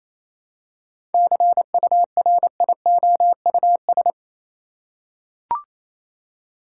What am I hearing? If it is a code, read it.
Morse code, heard as a keyed tone, about 28 wpm, 700 Hz. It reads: CURIOUS
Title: Seven Letter Words - 28wpm